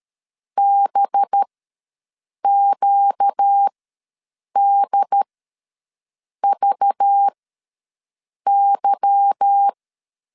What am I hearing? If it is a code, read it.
Morse code, heard as a keyed tone, about 13 wpm, spaced as for 6 wpm, 800 Hz. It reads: BQDVY
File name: LEZIONI / 8 YJ45 / 8 Y ws 12wpm 3409